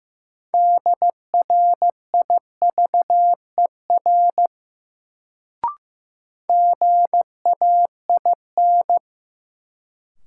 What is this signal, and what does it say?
Morse code, heard as a keyed tone, about 15 wpm, 700 Hz. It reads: DRIVER GAIN
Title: Sentences from Top 1000 Words - New Word Drill - 15wpm